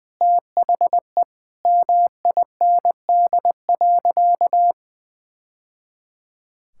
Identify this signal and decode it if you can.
Morse code, heard as a keyed tone, about 20 wpm, 700 Hz. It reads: THE MIND.
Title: Sentences from Top 1000 Words - Review Part 6 - 20wpm